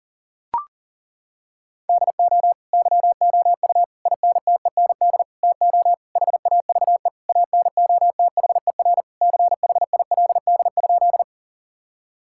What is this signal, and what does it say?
Morse code, heard as a keyed tone, about 37 wpm, 700 Hz. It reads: DO YOU INTEND TO HAVE ANOTHER CHILD?
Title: Sentences from Top 1600 Words 40wpm